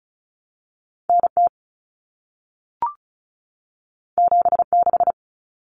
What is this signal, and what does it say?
Morse code, heard as a keyed tone, about 36 wpm, 700 Hz. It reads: NT 76
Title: Common Cut Numbers - 2-Digit 35wpm